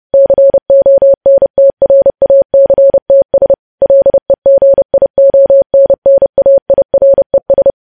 Morse code, heard as a keyed tone, about 30 wpm, 550 Hz. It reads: CONTRACTS LEGIONNAIRES
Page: news headlines in morse code at 30 wpm